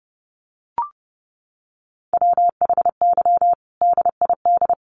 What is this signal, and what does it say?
Morse code, heard as a keyed tone, about 28 wpm, 700 Hz. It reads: WHY DID
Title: Sentences from Top 1400 Words 30wpm